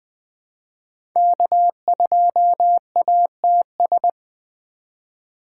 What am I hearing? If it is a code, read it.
Morse code, heard as a keyed tone, about 20 wpm, 700 Hz. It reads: K2ATS
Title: USA Call Signs - 3x Character Spacing - 20wpm